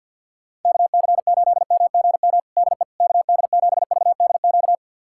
Morse code, heard as a keyed tone, about 52 wpm, 700 Hz. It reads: KC8MGM DE KD7VDQ
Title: USA Callsign DE Callsign - 6 Characters - 50wpm